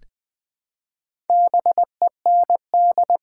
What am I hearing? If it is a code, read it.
Morse code, heard as a keyed tone, about 20 wpm, 700 Hz. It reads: BEND
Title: Build A Word - 1 Letter At A Time - 3 to 5-Letter Words - 20wpm